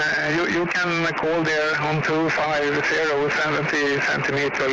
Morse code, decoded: none

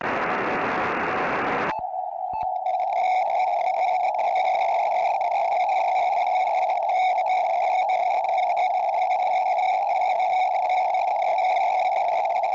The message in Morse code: none

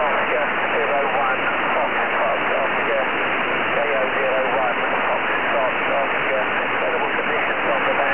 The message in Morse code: none

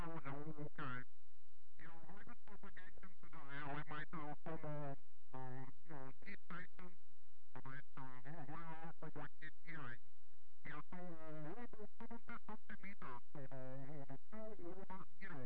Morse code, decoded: none

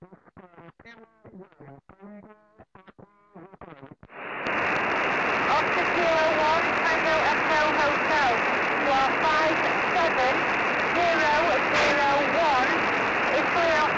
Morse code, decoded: none

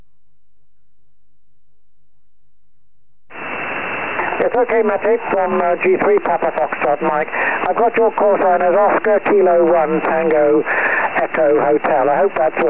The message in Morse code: none

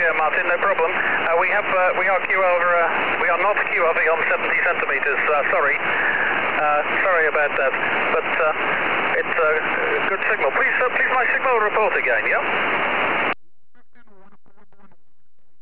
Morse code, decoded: none

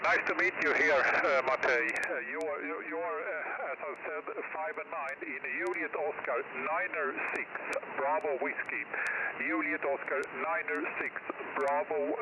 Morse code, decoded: none

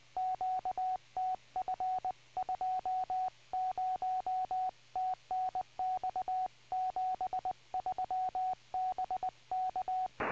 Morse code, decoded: QTF 20 TNX 73 BK

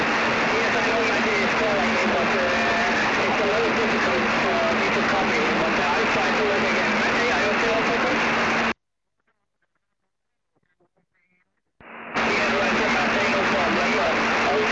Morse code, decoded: none

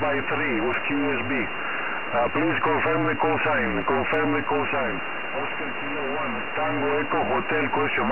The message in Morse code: none

